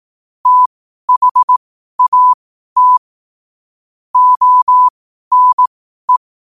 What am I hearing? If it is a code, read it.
Morse code, heard as a keyed tone, about 18 wpm, spaced as for 12 wpm, 1000 Hz. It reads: THAT ONE
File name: qotdfiles / 12wpm